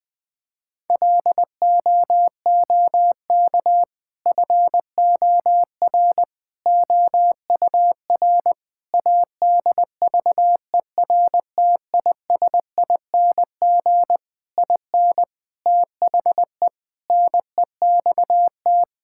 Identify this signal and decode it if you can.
Morse code, heard as a keyed tone, about 20 wpm, 700 Hz. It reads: LOOK FOR OUR ADVERTISING IN THE NEXT
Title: Sentences from Top 1000 Words - Review Part 6 - 20wpm